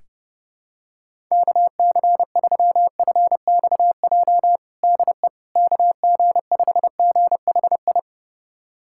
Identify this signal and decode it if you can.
Morse code, heard as a keyed tone, about 32 wpm, 700 Hz. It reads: KC3FXJ DE KG5GHI
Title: USA Callsign DE Callsign 30wpm